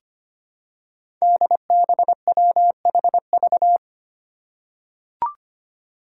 Morse code, heard as a keyed tone, about 25 wpm, 700 Hz. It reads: DBWHV
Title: Five Challenging Letter-Number Combinations 25wpm